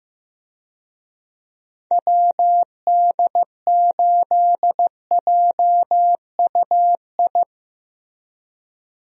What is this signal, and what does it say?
Morse code, heard as a keyed tone, about 15 wpm, 700 Hz. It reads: WD8JUI
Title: USA Call Signs - 6 Characters - Repeated 3x - 15wpm